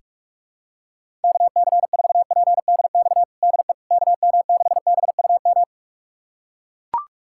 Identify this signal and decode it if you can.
Morse code, heard as a keyed tone, about 41 wpm, 700 Hz. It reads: KC4PDX DE KM6BUM
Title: USA Callsign DE Callsign 45wpm